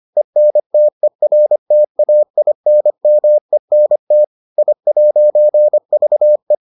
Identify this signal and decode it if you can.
Morse code, heard as a keyed tone, about 25 wpm, 600 Hz. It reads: ENTERTAINMENT I'VE